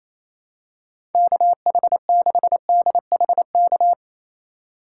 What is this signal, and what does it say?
Morse code, heard as a keyed tone, about 27 wpm, 700 Hz. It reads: KH6DHK
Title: K1USN SST QSO Practice - 28wpm